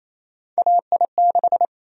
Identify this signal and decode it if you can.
Morse code, heard as a keyed tone, about 28 wpm, 700 Hz. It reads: AI6
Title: USA Call Sign Prefixes - Repeated 2x - 28wpm